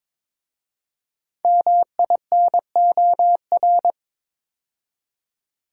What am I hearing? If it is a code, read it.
Morse code, heard as a keyed tone, about 23 wpm, 700 Hz. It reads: MINOR
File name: 5 Letter Words - First 3 Letters Given - 22wpm